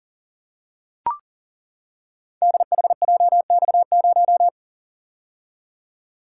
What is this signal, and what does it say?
Morse code, heard as a keyed tone, about 40 wpm, 700 Hz. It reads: DHJX0
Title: Code Groups - 5 Letter:Numbers - 40wpm